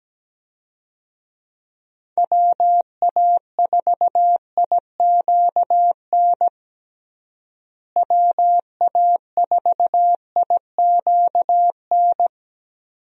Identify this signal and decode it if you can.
Morse code, heard as a keyed tone, about 17 wpm, 700 Hz. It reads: WA4IQN WA4IQN
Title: USA Call Signs - 6 Characters - Repeated 3x - 17wpm